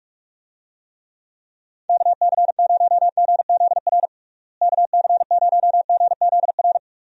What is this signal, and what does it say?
Morse code, heard as a keyed tone, about 45 wpm, 700 Hz. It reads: KC0GZR KC0GZR
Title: USA Call Signs - 6 Characters - Repeated 2x - 45wpm